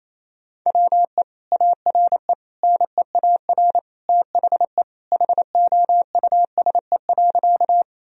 Morse code, heard as a keyed tone, about 28 wpm, 700 Hz. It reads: WE ARE NEAR THE HOUSE.
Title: Sentences from Top 1000 Words - Review Part 1 - 28wpm